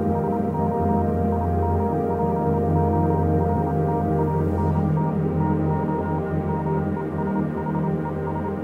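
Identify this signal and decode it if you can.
Morse code, heard as a keyed tone, about 27 wpm, 940 Hz, with noise. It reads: SO IMPORTANT TO ME SHES
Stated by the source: Morse code version of a music track.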